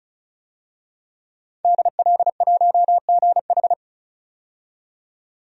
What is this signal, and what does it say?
Morse code, heard as a keyed tone, about 34 wpm, 700 Hz. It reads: DL1GH
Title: DXCC-100 Call Signs - 5 Characters - 35wpm